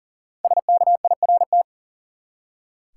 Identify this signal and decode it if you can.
Morse code, heard as a keyed tone, about 40 wpm, 700 Hz. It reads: SKIRT
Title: Five Letter Words - 40wpm